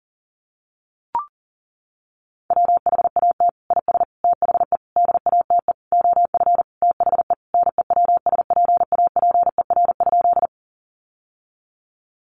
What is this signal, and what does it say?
Morse code, heard as a keyed tone, about 40 wpm, 700 Hz. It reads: WHAT IS THE DATE OF THE NEWSPAPER?
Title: Sentences from Top 1100 Words 40wpm